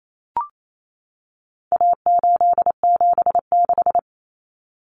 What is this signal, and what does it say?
Morse code, heard as a keyed tone, about 28 wpm, 700 Hz. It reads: A876
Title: Common Cut Numbers - 4-Digit 28wpm